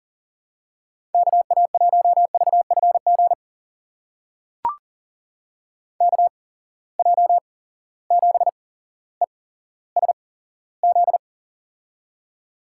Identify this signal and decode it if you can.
Morse code, heard as a keyed tone, about 40 wpm, spaced as for 17 wpm, 700 Hz. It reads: KA1VFG KJ7ESZ
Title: USA Call Signs - 6 Characters - 8x Character Spacing - 40wpm